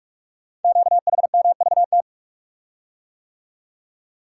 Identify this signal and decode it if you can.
Morse code, heard as a keyed tone, about 44 wpm, 690 Hz. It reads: QHMVT
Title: Code Groups - 5 Letters - 45wpm